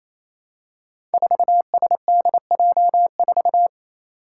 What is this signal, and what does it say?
Morse code, heard as a keyed tone, about 28 wpm, 700 Hz. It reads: 4SDJ4
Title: Five Challenging Letter-Number Combinations 28wpm